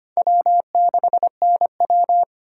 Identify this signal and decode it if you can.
Morse code, heard as a keyed tone, about 31 wpm, 700 Hz. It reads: W6NW
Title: Speed Racing - USA 1x2 Call Signs 25wpm